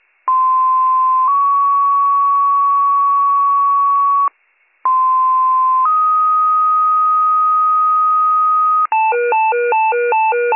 Two-tone page, heard as a one-second tone, then a three-second tone, about 1040 then 1120 Hz, about 1040 then 1250 Hz. It is heard from 0.3 to 8.9 s.